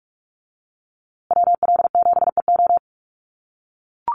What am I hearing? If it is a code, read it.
Morse code, heard as a keyed tone, about 44 wpm, 700 Hz. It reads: WL7EO
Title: USA Call Signs Repeated 3x 45wpm